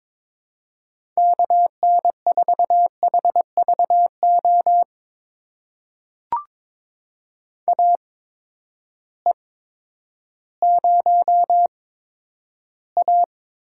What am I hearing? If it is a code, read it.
Morse code, heard as a keyed tone, about 21 wpm, spaced as for 11 wpm, 700 Hz. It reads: KN4HVO AE0A
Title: USA Call Signs - 8x Character Spacing - 22wpm